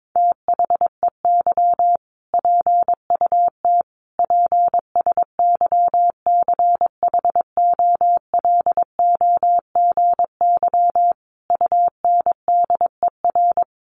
Morse code, heard as a keyed tone, about 22 wpm, 700 Hz. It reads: THEY PUT PSYCHOLOGY UNDER
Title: Sentences from Top 1400 Words 22wpm